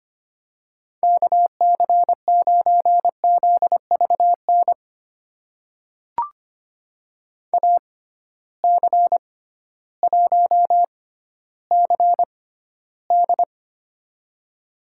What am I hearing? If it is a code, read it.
Morse code, heard as a keyed tone, about 25 wpm, spaced as for 13 wpm, 700 Hz. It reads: KC9ZVN AC1CD
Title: USA Call Signs - 6x Character Spacing - 25wpm